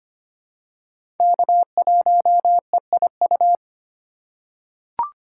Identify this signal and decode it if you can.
Morse code, heard as a keyed tone, about 25 wpm, 700 Hz. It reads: K1EIU
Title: USA Call Signs - 2x Character Spacing - 25wpm